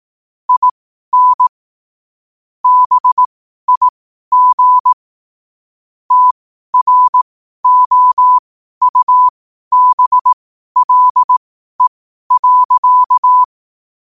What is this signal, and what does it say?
Morse code, heard as a keyed tone, about 18 wpm, spaced as for 12 wpm, 1000 Hz. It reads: IN BIG TROUBLE.
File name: 2026-01-07-12wpm